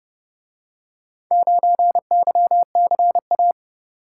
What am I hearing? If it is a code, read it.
Morse code, heard as a keyed tone, about 29 wpm, 700 Hz. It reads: 9YCA